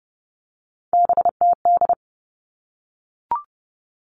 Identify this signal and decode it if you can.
Morse code, heard as a keyed tone, about 30 wpm, 700 Hz. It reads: BTD
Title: All Cut Numbers - 3-Digit 30wpm